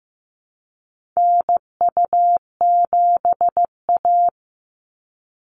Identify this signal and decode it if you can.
Morse code, heard as a keyed tone, about 15 wpm, 700 Hz. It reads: NU7A
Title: NA SOTA Chaser Callsigns 15wpm